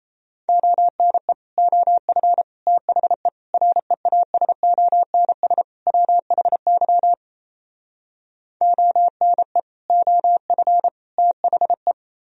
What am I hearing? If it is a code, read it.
Morse code, heard as a keyed tone, about 38 wpm, 700 Hz. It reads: ONE OF THE REASONS WHY ONE OF THE